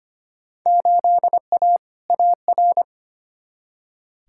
Morse code, heard as a keyed tone, about 25 wpm, 700 Hz. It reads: 8A AR
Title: Field Day Exchanges Encore - 25wpm